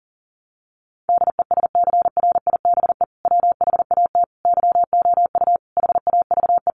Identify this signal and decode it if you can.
Morse code, heard as a keyed tone, about 40 wpm, 700 Hz. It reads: DESCRIBE WHAT YOU HAVE
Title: Sentences from Top 1000 Words 40wpm